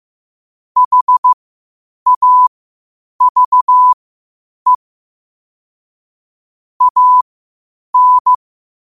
Morse code, heard as a keyed tone, about 15 wpm, spaced as for 8 wpm, 1000 Hz. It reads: HAVE AN
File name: qotdfiles / 8wpm